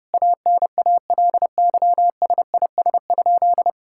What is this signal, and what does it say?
Morse code, heard as a keyed tone, about 32 wpm, 700 Hz. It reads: ANALYSIS?
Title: Sentences from Top 1000 Words - Review Part 5 - 30wpm